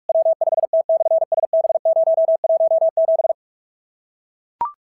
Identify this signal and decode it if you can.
Morse code, heard as a keyed tone, about 44 wpm, 690 Hz. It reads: W5T/SB017
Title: SOTA QSO Practice 45wpm